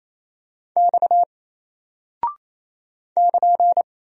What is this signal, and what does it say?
Morse code, heard as a keyed tone, about 27 wpm, 700 Hz. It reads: X <KN>
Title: Single Character (Letter:Number:Punctuation:Prosign) 28wpm